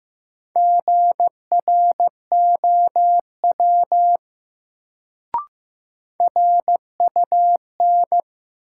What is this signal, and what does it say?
Morse code, heard as a keyed tone, about 15 wpm, 700 Hz. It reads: GROW RUN
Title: Sentences from Top 300 Words - New Word Drill - 15wpm